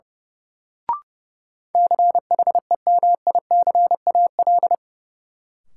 Morse code, heard as a keyed tone, about 30 wpm, 700 Hz. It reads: CHEMICAL
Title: Sentences from Top 1200 Words - New Word Drill - 30wpm